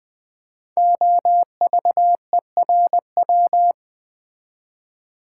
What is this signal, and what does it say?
Morse code, heard as a keyed tone, about 20 wpm, 700 Hz. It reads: OVERW
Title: Build A Word - 5 Letters At A Time - 12-Letter Words - 20wpm